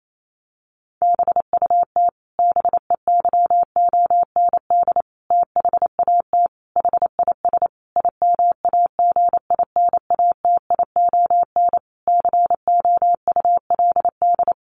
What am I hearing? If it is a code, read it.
Morse code, heard as a keyed tone, about 28 wpm, 700 Hz. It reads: BUT BEYOND THAT HIS IMAGINATION COULD
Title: Sentences from Top 1300 Words 28wpm